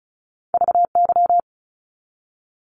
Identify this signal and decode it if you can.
Morse code, heard as a keyed tone, about 34 wpm, 700 Hz. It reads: VY
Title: Abbreviations 35wpm